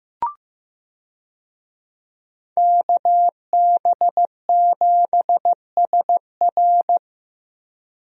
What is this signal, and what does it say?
Morse code, heard as a keyed tone, about 15 wpm, 700 Hz. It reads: KB7SR
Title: USA Call Signs - 5 Characters - Repeated 3x - 15wpm